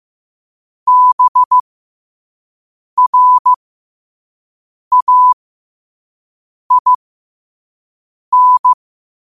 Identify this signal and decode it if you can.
Morse code, heard as a keyed tone, about 15 wpm, spaced as for 5 wpm, 1000 Hz. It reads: BRAIN
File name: qotdfiles / 5wpm